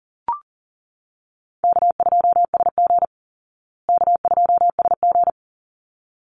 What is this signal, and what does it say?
Morse code, heard as a keyed tone, about 40 wpm, 700 Hz. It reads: K2SG K2SG
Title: Call Signs Repeated 3x - 40wpm